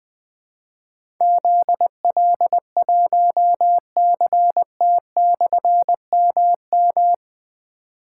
Morse code, heard as a keyed tone, about 20 wpm, 700 Hz. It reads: ZL1CT/MM
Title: Difficult Call Signs 20wpm